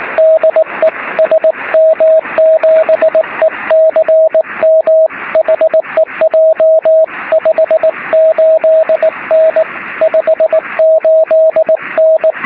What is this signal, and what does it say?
Morse code, heard as a keyed tone, about 19 wpm, 640 Hz, with radio noise. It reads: DE SM7ECM HEJ 58N 58N
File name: sm7lcb / 03-sm7ecm 23cm